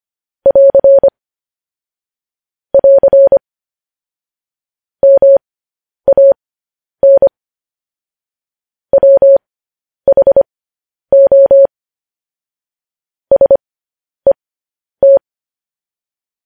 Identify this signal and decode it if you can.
Morse code, heard as a keyed tone, about 25 wpm, spaced as for 10 wpm, 550 Hz. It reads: <AR> <AR> MAN WHO SET
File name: podcast / News headlines-1x-25x10